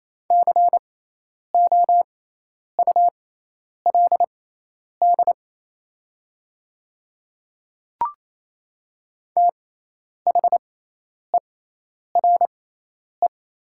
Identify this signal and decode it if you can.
Morse code, heard as a keyed tone, about 27 wpm, spaced as for 10 wpm, 700 Hz. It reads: COULD THERE
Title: 6x Character Spacing - Top 100 Words - 28wpm